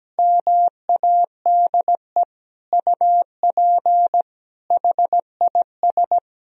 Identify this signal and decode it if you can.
Morse code, heard as a keyed tone, about 17 wpm, 700 Hz. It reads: MADE UP HIS
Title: Sentences from Top 1000 Words - Review Part 2 - 17wpm